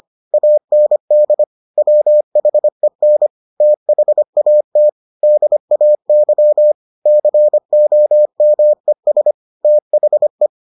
Morse code, heard as a keyed tone, about 25 wpm, 600 Hz. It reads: AND WHEN THAT DAY COMES THE